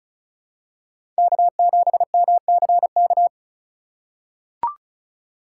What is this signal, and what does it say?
Morse code, heard as a keyed tone, about 36 wpm, 700 Hz. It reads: K7MCK
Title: USA Call Signs - 5 Characters - 3x Character Spacing - 35wpm